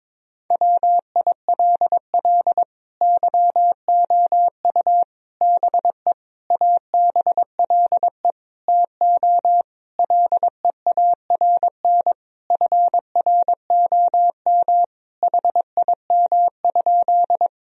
Morse code, heard as a keyed tone, about 22 wpm, 700 Hz. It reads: WILL YOU BE ABLE TO LEARN FROM HIM?
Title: Sentences from Top 1000 Words - Review Part 2 - 22wpm